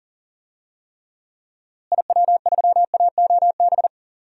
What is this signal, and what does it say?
Morse code, heard as a keyed tone, about 38 wpm, 700 Hz. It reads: IW3AOB
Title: DXCC-100 Call Signs - 6 Characters - Repeated 2x - 40wpm